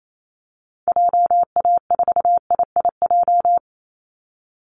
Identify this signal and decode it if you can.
Morse code, heard as a keyed tone, about 28 wpm, 700 Hz. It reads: JA4IIJ
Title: Call Signs - 28wpm